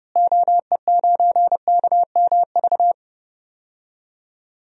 Morse code, heard as a keyed tone, about 30 wpm, 700 Hz. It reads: OE9KMV
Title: DXCC-100 Call Signs - 6 Characters - Repeated 3x - 30wpm